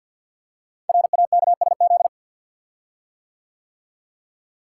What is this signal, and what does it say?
Morse code, heard as a keyed tone, about 50 wpm, 700 Hz. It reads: RAKSZ
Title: Code Groups - 5 Letter:Numbers - 50wpm